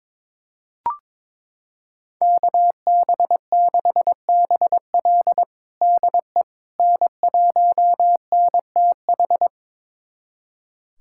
Morse code, heard as a keyed tone, about 21 wpm, 700 Hz. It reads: KB6BL DE N1NTH
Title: USA Callsign DE Callsign - 5 Characters - 22wpm